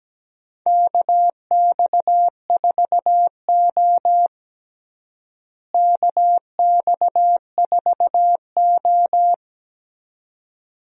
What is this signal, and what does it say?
Morse code, heard as a keyed tone, about 17 wpm, 700 Hz. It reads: KX4O KX4O
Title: USA Call Signs - 4 Characters - Repeated 2x - 17wpm